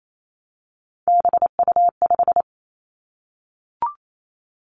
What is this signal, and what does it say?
Morse code, heard as a keyed tone, about 28 wpm, 690 Hz. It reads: BU5
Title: Three Challenging Letter-Number Combinations 28wpm